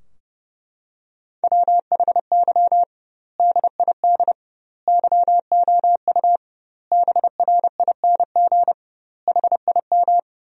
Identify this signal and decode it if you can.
Morse code, heard as a keyed tone, about 30 wpm, 700 Hz. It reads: WHY DID YOU BRING HIM
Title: 2x Word-Spacing - Sentences from Top 500 Words - 30wpm